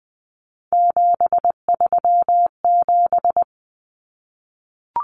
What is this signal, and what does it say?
Morse code, heard as a keyed tone, about 20 wpm, 700 Hz. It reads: 737
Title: Numbers - 3-Digit 20wpm